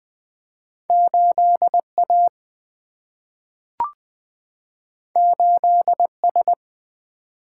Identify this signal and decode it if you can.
Morse code, heard as a keyed tone, about 20 wpm, 700 Hz. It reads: 8A 8S
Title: Single-Letter Single-Number 20wpm